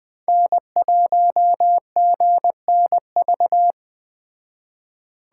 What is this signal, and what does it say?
Morse code, heard as a keyed tone, about 21 wpm, 700 Hz. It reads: N1GNV
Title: Speed Racing - USA 1x3 Call Signs 17wpm